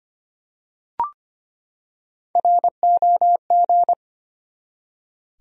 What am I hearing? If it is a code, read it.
Morse code, heard as a keyed tone, about 25 wpm, 700 Hz. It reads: ROG
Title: USA Common Names - Max 3 Letters - 25wpm